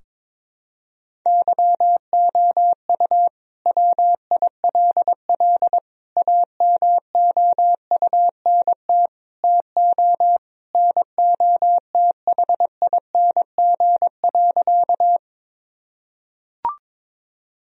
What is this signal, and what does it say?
Morse code, heard as a keyed tone, about 21 wpm, 700 Hz. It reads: YOU WILL AMOUNT TO NOTHING.
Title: Sentences from Top 1000 Words - Review Part 4 - 22wpm